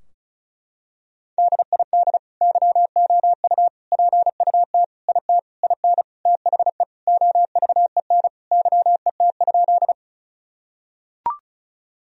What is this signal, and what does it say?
Morse code, heard as a keyed tone, about 36 wpm, 700 Hz. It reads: DID YOU PUT IT IN THE OVEN YET?
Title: Sentences from Top 1000 Words - Review Part 4 - 35wpm